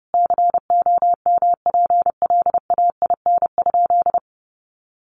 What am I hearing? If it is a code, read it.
Morse code, heard as a keyed tone, about 28 wpm, 700 Hz. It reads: COMPLAIN?
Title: Sentences from Top 1300 Words 30wpm